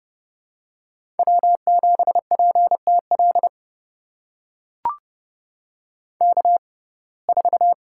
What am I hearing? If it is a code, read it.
Morse code, heard as a keyed tone, about 29 wpm, spaced as for 22 wpm, 700 Hz. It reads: W7PTL K4
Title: USA Call Signs - 5 Characters - 6x Character Spacing - 30wpm